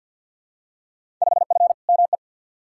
Morse code, heard as a keyed tone, about 53 wpm, 700 Hz. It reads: 5F ME
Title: Field Day Exchanges Encore - 50wpm